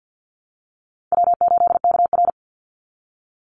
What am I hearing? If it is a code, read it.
Morse code, heard as a keyed tone, about 50 wpm, 700 Hz. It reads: W8KR
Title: Easy Call Signs Repeated 2x - 50wpm